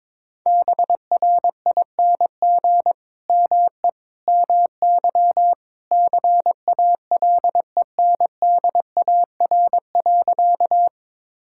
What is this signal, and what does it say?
Morse code, heard as a keyed tone, about 22 wpm, 700 Hz. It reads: BRING ME MY CALENDAR.
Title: Sentences from Top 1600 Words 22wpm